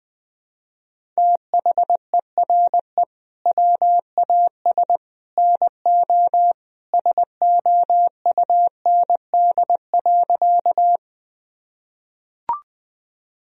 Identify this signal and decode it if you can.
Morse code, heard as a keyed tone, about 20 wpm, 700 Hz. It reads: THERE WAS NO SOUND.
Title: Sentences from Top 1000 Words - Review Part 1 - 20wpm